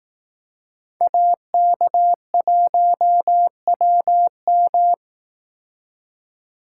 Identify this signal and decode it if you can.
Morse code, heard as a keyed tone, about 18 wpm, 700 Hz. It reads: AK1WM
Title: Speed Racing - USA 2x2 Call Signs 15wpm